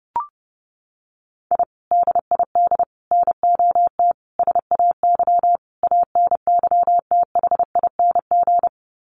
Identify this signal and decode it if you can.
Morse code, heard as a keyed tone, about 30 wpm, 700 Hz. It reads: I DID NOT SAY ANYTHING
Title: Sets of 5 Words Encore2 30wpm